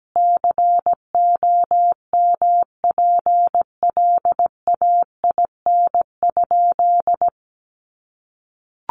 Morse code, heard as a keyed tone, about 17 wpm, 700 Hz. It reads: COMPLAIN?
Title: Sentences from Top 1300 Words 17wpm